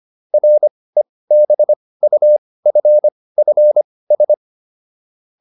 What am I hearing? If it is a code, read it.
Morse code, heard as a keyed tone, about 25 wpm, spaced as for 18 wpm, 600 Hz. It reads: REBUFFS